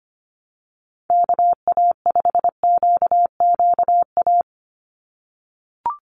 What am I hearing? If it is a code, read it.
Morse code, heard as a keyed tone, about 25 wpm, 700 Hz. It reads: KA5QQA
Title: NA SOTA Chaser Callsigns 25wpm